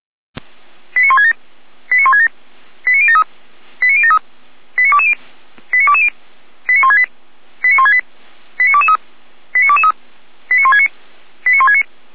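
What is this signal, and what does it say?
Five-tone selective call (ZVEI), heard as quick runs of five tones, tones of about 70 ms. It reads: 89167 89167 89073 89073 89220 89220 89168 89168 89223 89223 89169 89169
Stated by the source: ZVEI five-tone call 89123 received on SDR, Feuerwehr Orlamünde.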